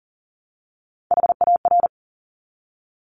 Morse code, heard as a keyed tone, about 40 wpm, 700 Hz. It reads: HAR
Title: Top Three Letter Combinations in Words 40wpm